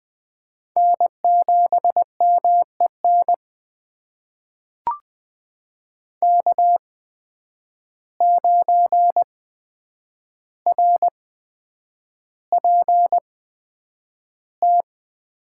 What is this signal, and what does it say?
Morse code, heard as a keyed tone, about 20 wpm, spaced as for 9 wpm, 700 Hz. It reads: N7MEN K9RPT
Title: USA Call Signs - 5 Characters - 8x Character Spacing - 20wpm